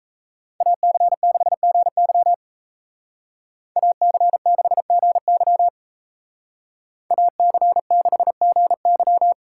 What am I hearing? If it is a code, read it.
Morse code, heard as a keyed tone, about 33 wpm, 700 Hz. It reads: AC6GY AC6GY AC6GY
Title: Speed Racing - USA 2x2 Call Signs 28wpm